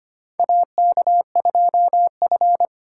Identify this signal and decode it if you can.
Morse code, heard as a keyed tone, about 25 wpm, 700 Hz. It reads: AK2F
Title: USA Call Signs - 2x Character Spacing - 25wpm